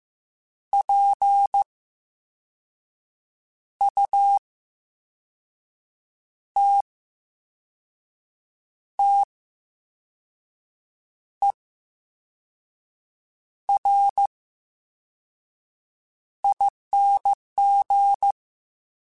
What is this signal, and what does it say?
Morse code, heard as a keyed tone, about 15 wpm, spaced as for 5 wpm, 800 Hz. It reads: PUTTERING
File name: Special INGs / Special INGs-mp3s / Slow ing @13